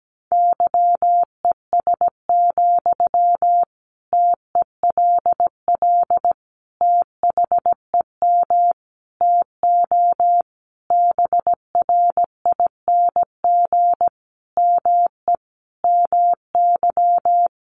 Morse code, heard as a keyed tone, about 17 wpm, 700 Hz. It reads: YES, TELL THEM TO BRING ME MY
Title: Sentences from Top 500 Words 17wpm